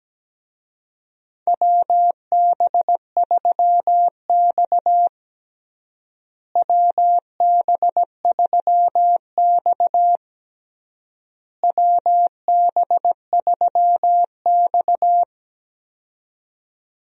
Morse code, heard as a keyed tone, about 17 wpm, 700 Hz. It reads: WB3X WB3X WB3X
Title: USA Call Signs - 4 Characters - Repeated 3x - 17wpm